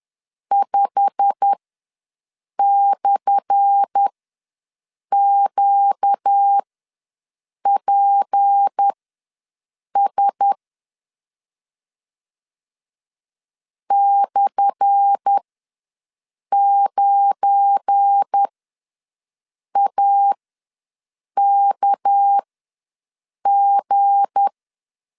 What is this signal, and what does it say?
Morse code, heard as a keyed tone, about 10 wpm, spaced as for 6 wpm, 800 Hz. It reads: 5/QPS /9AKG
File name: LEZIONI / 10 SIGNS / 10A SLASH ws 10wpm 4121